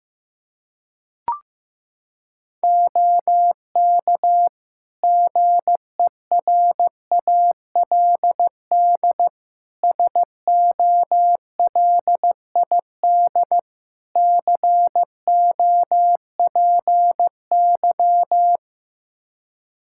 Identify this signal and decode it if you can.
Morse code, heard as a keyed tone, about 15 wpm, 700 Hz. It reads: OK GERALD SOLID COPY
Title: Practice QSO - Exchange 3 - More Info - 15wpm